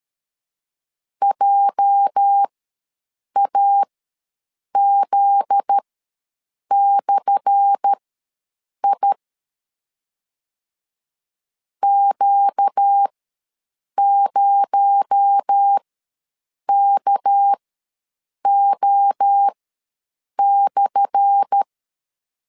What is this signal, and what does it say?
Morse code, heard as a keyed tone, about 13 wpm, spaced as for 7 wpm, 800 Hz. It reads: JAZ/I Q0KO/
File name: LEZIONI / 10 SIGNS / 10A SLASH ws 12wpm 3443